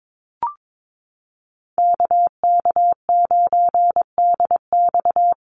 Morse code, heard as a keyed tone, about 22 wpm, 700 Hz. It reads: KK9DX
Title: Call Signs Repeated 2x - 22wpm